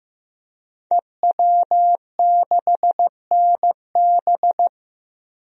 Morse code, heard as a keyed tone, about 15 wpm, 700 Hz. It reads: EW6NB
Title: DXCC-100 Call Signs - 5 Characters - Repeated 3x - 15wpm